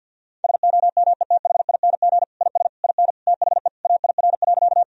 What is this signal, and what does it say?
Morse code, heard as a keyed tone, about 50 wpm, 700 Hz. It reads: SOMETHING IS IN THE AIR.
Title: Sentences from Top 1000 Words - Review Part 1 - 50wpm